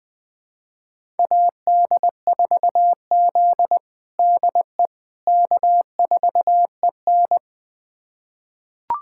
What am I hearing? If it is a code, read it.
Morse code, heard as a keyed tone, about 21 wpm, 700 Hz. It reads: AD4Z DE K4EN
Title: USA Callsign DE Callsign - 4 Characters - 20wpm